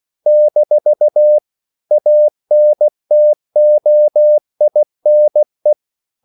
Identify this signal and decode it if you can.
Morse code, heard as a keyed tone, about 16 wpm, 600 Hz. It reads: - ANTOINE
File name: CW / Practice / EnglishQuotations0216